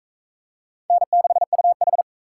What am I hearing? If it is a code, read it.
Morse code, heard as a keyed tone, about 39 wpm, 700 Hz. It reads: N6UH